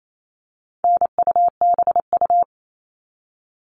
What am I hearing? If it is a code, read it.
Morse code, heard as a keyed tone, about 28 wpm, 700 Hz. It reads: NUBU
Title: All Cut Numbers - 4-Digit 28wpm